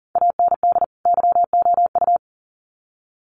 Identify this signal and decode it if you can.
Morse code, heard as a keyed tone, about 40 wpm, 700 Hz. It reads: AND YOU
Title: Sets of 2 Words 40wpm